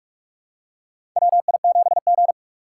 Morse code, heard as a keyed tone, about 46 wpm, 700 Hz. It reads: WI7G
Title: USA Call Signs - 4 Characters - 6x Character Spacing - 45wpm